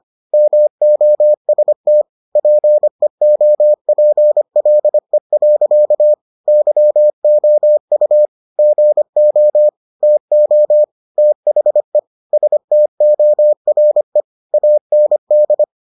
Morse code, heard as a keyed tone, about 25 wpm, 600 Hz. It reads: MOST PEOPLE. YOU GO TO THE STORE AND